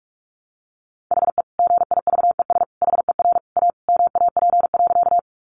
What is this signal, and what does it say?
Morse code, heard as a keyed tone, about 44 wpm, 700 Hz. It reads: HE GIVES HER A MAP.
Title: Additional Easy Phrases 45wpm